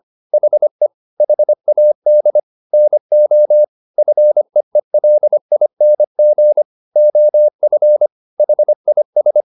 Morse code, heard as a keyed tone, about 25 wpm, 600 Hz. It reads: HE HAD NO FEELING OF HIS